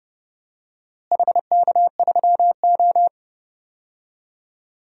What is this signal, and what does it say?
Morse code, heard as a keyed tone, about 30 wpm, 700 Hz. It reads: HK3O
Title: DXCC-100 Call Signs - 4 Characters - Repeated 2x - 30wpm